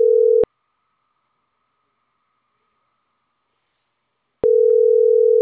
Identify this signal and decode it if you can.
Ringback tone (North America), 440 + 480 Hz.